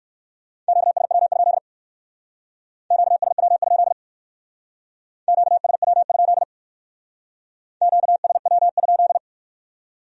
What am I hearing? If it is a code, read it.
Morse code, heard as a keyed tone, about 58 wpm, 700 Hz. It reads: QSW? QSW? QSW? QSW?